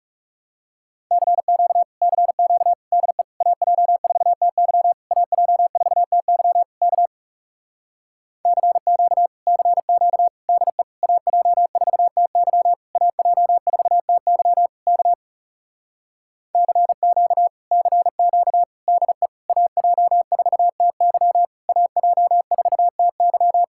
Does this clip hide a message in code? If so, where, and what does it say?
Yes: Morse code from 1.1 to 23.6 s, heard as a keyed tone, about 40 wpm, 700 Hz. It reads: CQ CQ DE AJ4TY AJ4TY K CQ CQ DE AJ4TY AJ4TY K CQ CQ DE AJ4TY AJ4TY